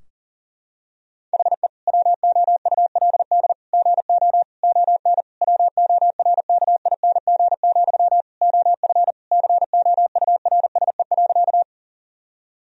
Morse code, heard as a keyed tone, about 40 wpm, 700 Hz. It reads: HE WOULD GO ON WORKING, OF COURSE.
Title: Sentences from Top 1000 Words - Review Part 4 - 40wpm